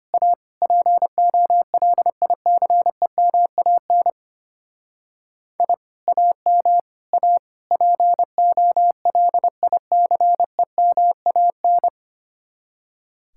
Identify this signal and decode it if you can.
Morse code, heard as a keyed tone, about 31 wpm, 700 Hz. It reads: A POLICEMAN I AM A POLICEMAN
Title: Speed-Racing Common QSO Phrases - 25wpm